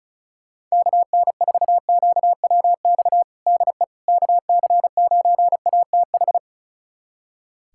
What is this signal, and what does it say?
Morse code, heard as a keyed tone, about 34 wpm, 700 Hz. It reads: KN4QWX DE KC9ATH
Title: USA Callsign DE Callsign - 6 Characters - 35wpm